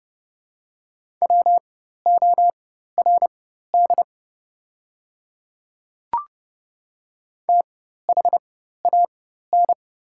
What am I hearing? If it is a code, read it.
Morse code, heard as a keyed tone, about 30 wpm, spaced as for 14 wpm, 690 Hz. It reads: WORD THAN
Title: 4x Character Spacing - Top 100 Words - 30wpm